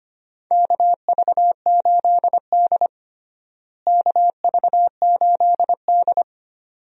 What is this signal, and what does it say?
Morse code, heard as a keyed tone, about 25 wpm, 700 Hz. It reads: KV8D KV8D
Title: USA Call Signs - 4 Characters - Repeated 2x - 25wpm